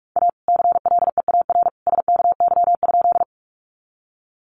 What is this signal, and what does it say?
Morse code, heard as a keyed tone, about 44 wpm, 700 Hz. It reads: A CLEAR SKY?
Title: Sentences from Top 1200 Words 45wpm